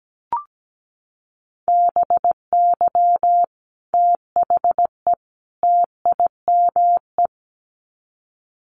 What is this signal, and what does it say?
Morse code, heard as a keyed tone, about 17 wpm, 700 Hz. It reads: BY THE TIME
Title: Sets of 3 Words 17wpm